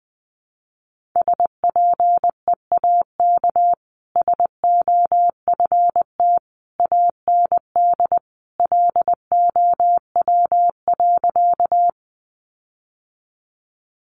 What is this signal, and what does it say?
Morse code, heard as a keyed tone, about 20 wpm, 700 Hz. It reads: SPEAK SOFT AND LOW.
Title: Sentences from Top 1500 Words 20wpm